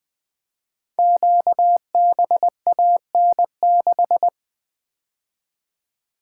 Morse code, heard as a keyed tone, about 20 wpm, 700 Hz. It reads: QBAN6
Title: Code Groups - 5 Letter:Numbers - 20wpm